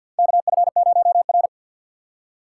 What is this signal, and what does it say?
Morse code, heard as a keyed tone, about 51 wpm, 700 Hz. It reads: KF0R
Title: USA Call Signs - 4 Characters - 2x Character Spacing - 50wpm